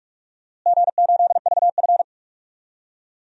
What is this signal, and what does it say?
Morse code, heard as a keyed tone, about 46 wpm, 700 Hz. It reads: G8VF
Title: DXCC-100 Call Signs - 4 Characters - Repeated 2x - 45wpm